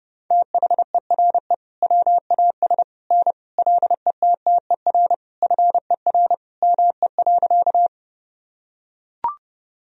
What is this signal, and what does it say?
Morse code, heard as a keyed tone, about 32 wpm, 700 Hz. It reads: THERE WAS N LETTER FER ME.
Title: Sentences from Top 300 Words and Abbreviations 30wpm